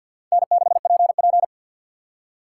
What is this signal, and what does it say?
Morse code, heard as a keyed tone, about 53 wpm, 700 Hz. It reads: N6PP
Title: USA Call Signs - 4 Characters - 6x Character Spacing - 50wpm